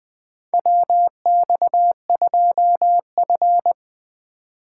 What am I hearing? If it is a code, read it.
Morse code, heard as a keyed tone, about 20 wpm, 700 Hz. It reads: WX2F
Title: USA Call Signs - 4 Characters - 3x Character Spacing - 20wpm